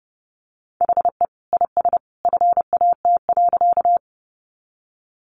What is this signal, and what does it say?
Morse code, heard as a keyed tone, about 30 wpm, 700 Hz. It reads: HE IS FAT.
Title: Easy Phrases 30wpm